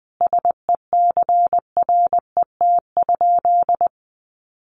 Morse code, heard as a keyed tone, about 20 wpm, 700 Hz. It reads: SECRET?
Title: Sentences from Top 1200 Words 20wpm